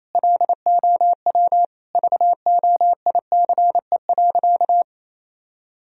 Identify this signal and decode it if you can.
Morse code, heard as a keyed tone, about 28 wpm, 700 Hz. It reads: LOW VOICE.
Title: Sentences from Top 1000 Words - Review Part 1 - 28wpm